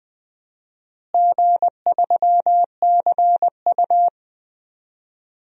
Morse code, heard as a keyed tone, about 20 wpm, 700 Hz. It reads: G3CU
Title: DXCC-100 Call Signs - 4 Characters - Repeated 3x - 20wpm